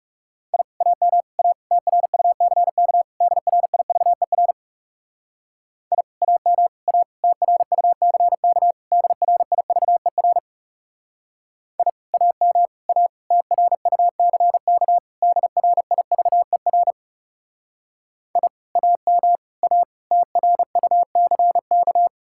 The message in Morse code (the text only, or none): I AM A TRUCK DRIVER I AM A TRUCK DRIVER I AM A TRUCK DRIVER I AM A TRUCK